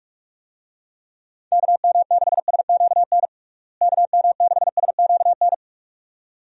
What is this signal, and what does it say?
Morse code, heard as a keyed tone, about 45 wpm, 700 Hz. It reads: KM6SQN KM6SQN
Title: USA Call Signs - 6 Characters - Repeated 2x - 45wpm